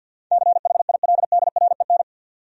Morse code, heard as a keyed tone, about 68 wpm, spaced as for 59 wpm, 700 Hz. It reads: CHILDREN
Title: Speed-Racing Top 300 Words 50wpm